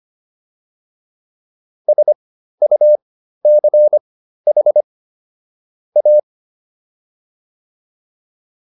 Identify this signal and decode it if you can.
Morse code, heard as a keyed tone, about 25 wpm, spaced as for 13 wpm, 600 Hz. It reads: SUCH A